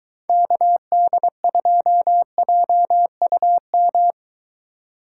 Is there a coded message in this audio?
Yes: Morse code from 0.3 to 4.1 s, heard as a keyed tone, about 22 wpm, 700 Hz. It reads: KD2JUM